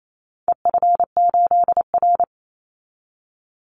Morse code, heard as a keyed tone, about 28 wpm, 700 Hz. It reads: EF8R
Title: Call Signs Repeated 2x - 28wpm